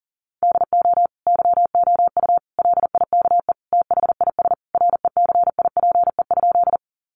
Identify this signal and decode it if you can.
Morse code, heard as a keyed tone, about 40 wpm, 700 Hz. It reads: DO YOU LIKE THIS RECIPE?
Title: Sentences from Top 1200 Words 40wpm